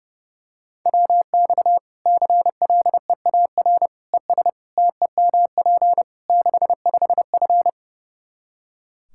Morse code, heard as a keyed tone, about 30 wpm, 700 Hz. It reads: WX CLEAR ES TEMP 65F
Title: Practice QSO - Exchange 2 - Info - 30wpm